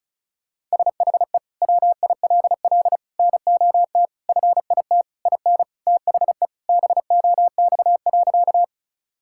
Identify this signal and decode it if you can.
Morse code, heard as a keyed tone, about 36 wpm, 700 Hz. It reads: SHE WILL NOT FIT IN THE BOX.
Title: Sentences from Top 1000 Words - Review Part 5 - 35wpm